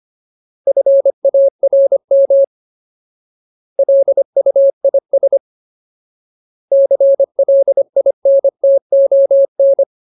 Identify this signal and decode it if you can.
Morse code, heard as a keyed tone, about 25 wpm, 550 Hz. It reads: FARM LUIS CLINTON
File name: Ultimate Word List / 25WPM Ultimate